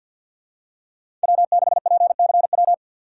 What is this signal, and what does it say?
Morse code, heard as a keyed tone, about 50 wpm, 700 Hz. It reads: W6PCW
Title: USA Call Signs - 5 Characters - 6x Character Spacing - 50wpm